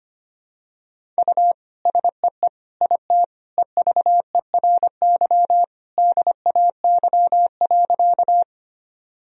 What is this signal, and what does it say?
Morse code, heard as a keyed tone, about 25 wpm, 700 Hz. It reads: U SEE IT EVERY DAY.
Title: Sentences from Top 200 Words and Abbreviations 25wpm